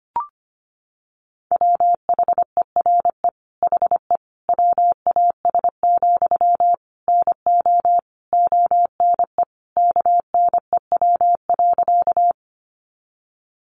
Morse code, heard as a keyed tone, about 25 wpm, 700 Hz. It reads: WHERE HE WAS, NO ONE KNEW.
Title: Sentences from Top 400 Words 25wpm